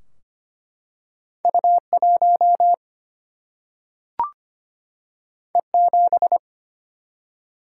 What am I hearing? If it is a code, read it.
Morse code, heard as a keyed tone, about 25 wpm, 700 Hz. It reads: U1 E7
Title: Single-Letter Single-Number 25wpm